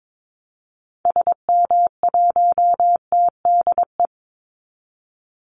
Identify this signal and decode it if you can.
Morse code, heard as a keyed tone, about 22 wpm, 700 Hz. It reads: SM1TDE
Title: Call Signs Repeated 2x - 22wpm